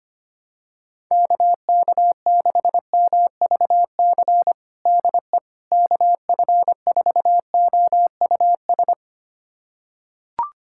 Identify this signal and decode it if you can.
Morse code, heard as a keyed tone, about 25 wpm, 700 Hz. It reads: KK6MVC DE KF4OUS
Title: USA Callsign DE Callsign - 6 Characters - 25wpm